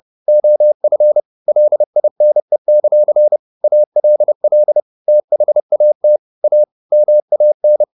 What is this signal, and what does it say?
Morse code, heard as a keyed tone, about 30 wpm, 600 Hz. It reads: OF LINE; ALL THAT A MAN
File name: the old man and the sea / the old man and the sea 1 30wpm